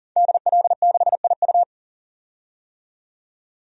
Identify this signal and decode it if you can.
Morse code, heard as a keyed tone, about 37 wpm, 700 Hz. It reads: DL6IU
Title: DXCC-100 Call Signs - 5 Characters - 40wpm